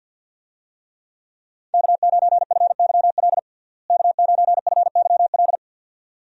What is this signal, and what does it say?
Morse code, heard as a keyed tone, about 52 wpm, 700 Hz. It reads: K9FYL K9FYL
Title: USA Call Signs - 5 Characters - Repeated 2x - 50wpm